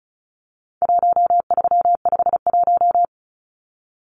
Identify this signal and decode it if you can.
Morse code, heard as a keyed tone, about 34 wpm, 700 Hz. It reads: 1351
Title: Numbers - 4-Digit 35wpm